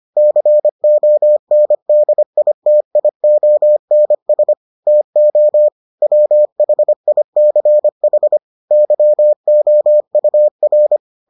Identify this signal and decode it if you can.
Morse code, heard as a keyed tone, about 25 wpm, 600 Hz. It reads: CONDITIONS TO WHICH YOUR